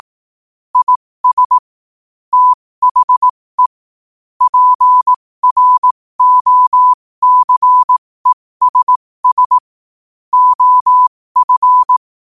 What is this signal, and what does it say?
Morse code, heard as a keyed tone, about 18 wpm, spaced as for 15 wpm, 1000 Hz. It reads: IS THE PROCESS OF
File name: qotdfiles / 15wpm